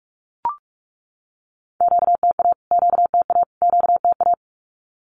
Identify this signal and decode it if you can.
Morse code, heard as a keyed tone, about 44 wpm, 700 Hz. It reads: QTA QTA QTA
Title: Q-Codes Repeated 3x 45wpm